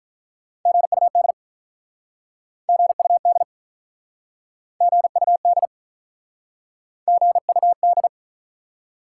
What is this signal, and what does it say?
Morse code, heard as a keyed tone, about 48 wpm, 700 Hz. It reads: GUD GUD GUD GUD